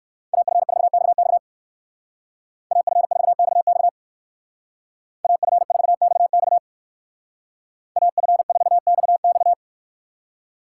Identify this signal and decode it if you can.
Morse code, heard as a keyed tone, about 59 wpm, 700 Hz. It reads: AF4XX AF4XX AF4XX AF4XX